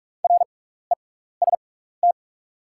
Morse code, heard as a keyed tone, about 44 wpm, spaced as for 15 wpm, 700 Hz. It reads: REST
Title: 6x Character Spacing - Top 500 Words - 45wpm